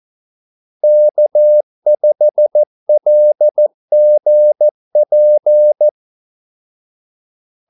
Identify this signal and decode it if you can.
Morse code, heard as a keyed tone, about 14 wpm, 600 Hz. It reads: K5LGP
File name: Callsigns-14wpm